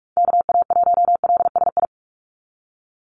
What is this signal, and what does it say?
Morse code, heard as a keyed tone, about 46 wpm, 690 Hz. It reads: KA1LSI
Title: USA Call Signs 45wpm